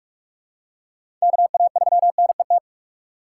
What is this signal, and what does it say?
Morse code, heard as a keyed tone, about 46 wpm, 690 Hz. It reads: KA3NET
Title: USA Call Signs - 6 Characters - 3x Character Spacing - 45wpm